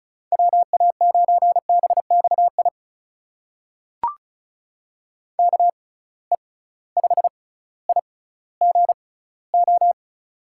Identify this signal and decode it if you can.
Morse code, heard as a keyed tone, about 36 wpm, spaced as for 18 wpm, 700 Hz. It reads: WA9BXI KE5IGO
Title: USA Call Signs - 6 Characters - 6x Character Spacing - 35wpm